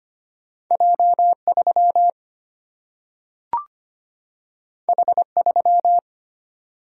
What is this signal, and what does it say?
Morse code, heard as a keyed tone, about 25 wpm, 700 Hz. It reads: J 3 H 3
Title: Single-Letter Single-Number 25wpm